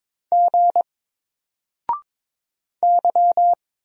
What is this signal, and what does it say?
Morse code, heard as a keyed tone, about 22 wpm, 700 Hz. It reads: G Y
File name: Single-Letter-Number-22wpm